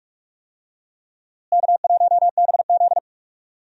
Morse code, heard as a keyed tone, about 46 wpm, 700 Hz. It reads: K1BZ
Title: USA Call Signs - 4 Characters - Repeated 2x - 45wpm